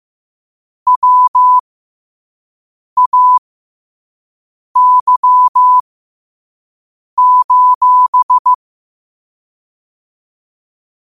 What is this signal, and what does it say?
Morse code, heard as a keyed tone, about 15 wpm, spaced as for 5 wpm, 1000 Hz. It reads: WAY:
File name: qotdfiles / 5wpm